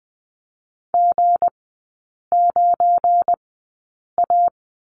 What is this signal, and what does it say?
Morse code, heard as a keyed tone, about 20 wpm, spaced as for 8 wpm, 700 Hz. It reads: G9A
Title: 172 Lesson 28e - Number 9 - Three Characters 20wpm